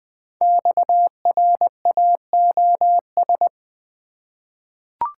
Morse code, heard as a keyed tone, about 20 wpm, 700 Hz. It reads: XRAOS
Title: Code Groups - 5 Letters - 20wpm